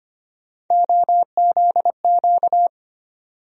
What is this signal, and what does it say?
Morse code, heard as a keyed tone, about 25 wpm, 700 Hz. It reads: OZQ
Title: USA Call Sign Postfixes - 3 Letters 25wpm